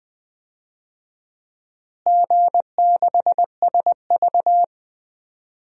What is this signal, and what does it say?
Morse code, heard as a keyed tone, about 20 wpm, 700 Hz. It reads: G6SV
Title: DXCC-100 Call Signs - 4 Characters - Repeated 2x - 20wpm